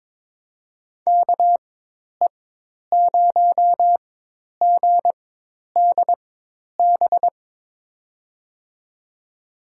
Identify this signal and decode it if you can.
Morse code, heard as a keyed tone, about 22 wpm, spaced as for 10 wpm, 700 Hz. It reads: KE0GDB
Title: USA Call Signs - 4x Character Spacing - 22wpm